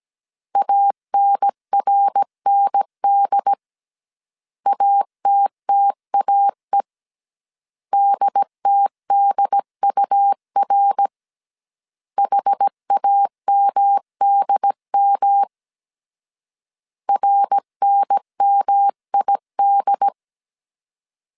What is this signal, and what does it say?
Morse code, heard as a keyed tone, about 17 wpm, 800 Hz. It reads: ANRND ATTAE DTDUR HAMDM RNMID